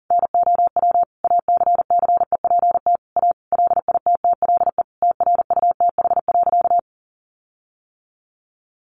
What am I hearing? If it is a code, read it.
Morse code, heard as a keyed tone, about 40 wpm, 700 Hz. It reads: NOW ACCEPT A LITTLE TRUTH.
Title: Sentences from Top 1300 Words 40wpm